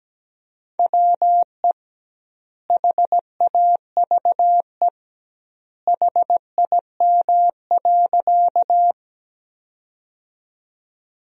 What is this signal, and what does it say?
Morse code, heard as a keyed tone, about 17 wpm, 700 Hz. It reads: WE HAVE HIM.
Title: 2x Word-Spacing - Sentences from Top 100 Words - 17wpm